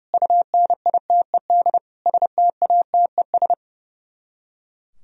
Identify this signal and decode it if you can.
Morse code, heard as a keyed tone, about 40 wpm, 700 Hz. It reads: UNITED STATES